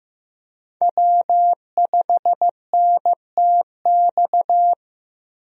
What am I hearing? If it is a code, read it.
Morse code, heard as a keyed tone, about 15 wpm, 700 Hz. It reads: W5NTX
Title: USA Call Signs - 5 Characters - Repeated 2x - 15wpm